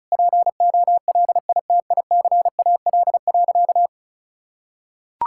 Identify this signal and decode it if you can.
Morse code, heard as a keyed tone, about 34 wpm, 700 Hz. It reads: POLITICAL.
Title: Sentences from Top 1600 Words 35wpm